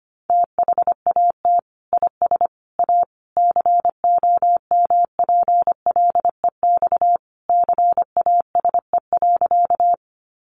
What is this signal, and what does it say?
Morse code, heard as a keyed tone, about 25 wpm, 700 Hz. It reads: THAT IS A COMPLEX CASE.
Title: Sentences from Top 900 Words 25wpm